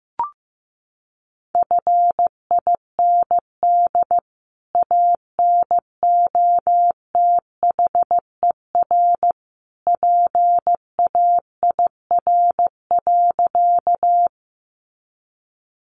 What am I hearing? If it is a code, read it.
Morse code, heard as a keyed tone, about 15 wpm, 700 Hz. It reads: FIND ANOTHER PAIR.
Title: Sentences from Top 1200 Words 15wpm